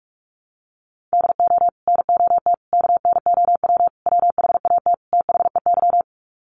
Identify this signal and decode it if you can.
Morse code, heard as a keyed tone, about 44 wpm, 700 Hz. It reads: DO NOT KNOW WHAT THEY
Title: Sets of 5 Words 45wpm